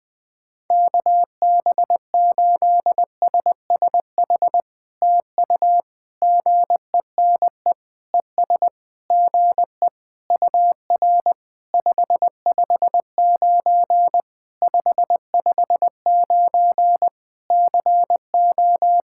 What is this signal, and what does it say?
Morse code, heard as a keyed tone, about 20 wpm, 700 Hz. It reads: KB8SSH TU GENE ES GE UR 559 559 CO
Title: POTA QSO Practice - 20wpm